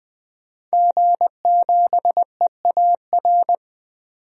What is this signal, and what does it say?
Morse code, heard as a keyed tone, about 21 wpm, 700 Hz. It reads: G7EAR
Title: DXCC-100 Call Signs - 5 Characters - 20wpm